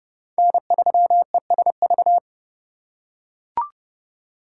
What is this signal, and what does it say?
Morse code, heard as a keyed tone, about 28 wpm, 700 Hz. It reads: N3ESV